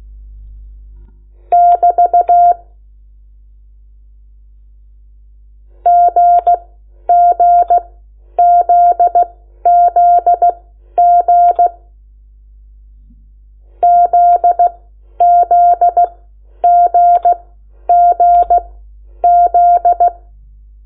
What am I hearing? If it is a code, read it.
Morse code, heard as a keyed tone, about 16 wpm, spaced as for 10 wpm, 700 Hz. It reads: <BT> GGZZG ZZGGZ
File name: Lecciones / CLASE Nº8